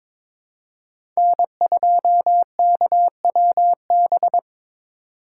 Morse code, heard as a keyed tone, about 22 wpm, 700 Hz. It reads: N2KWB
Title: USA Call Signs - 5 Characters - 4x Character Spacing - 22wpm